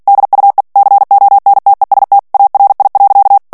Morse code, heard as a keyed tone, about 48 wpm, 800 Hz. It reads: DUE CONTEST ARI.
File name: ITALIEN texte clair 2 vitesse 45 mots minute